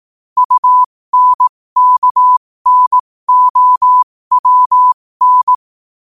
Morse code, heard as a keyed tone, about 18 wpm, spaced as for 15 wpm, 1000 Hz. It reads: UNKNOWN
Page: New quotes every day in morse code at 15 Words per minute.